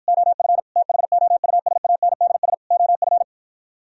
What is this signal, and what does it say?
Morse code, heard as a keyed tone, about 52 wpm, 700 Hz. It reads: OF THOUSANDS OF